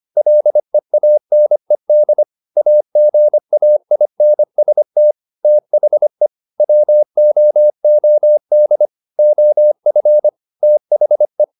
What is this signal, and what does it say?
Morse code, heard as a keyed tone, about 25 wpm, 600 Hz. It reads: LEANED AGAINST THE WOOD OF THE
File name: the old man and the sea part 1 25wpm